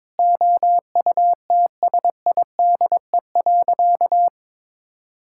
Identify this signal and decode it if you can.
Morse code, heard as a keyed tone, about 22 wpm, 700 Hz. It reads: OUTSIDE.
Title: Sentences from Top 1000 Words - Review Part 6 - 22wpm